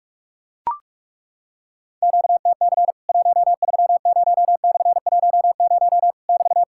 Morse code, heard as a keyed tone, about 42 wpm, 700 Hz. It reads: QTC 130/10 <BT>
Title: Worked All Europe DX Contest - QTC Exchange - 45wpm